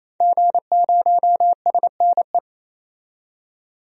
Morse code, heard as a keyed tone, about 28 wpm, 700 Hz. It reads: G0SNE
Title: DXCC-100 Call Signs - 5 Characters - 28wpm